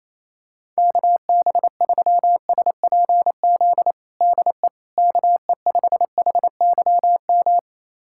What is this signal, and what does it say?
Morse code, heard as a keyed tone, about 28 wpm, 700 Hz. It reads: KB3SPZ DE KE5HYM
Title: USA Callsign DE Callsign - 6 Characters - 28wpm